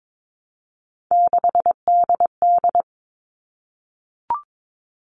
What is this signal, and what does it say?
Morse code, heard as a keyed tone, about 22 wpm, 700 Hz. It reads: 6DD
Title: All Cut Numbers - 3-Digit 22wpm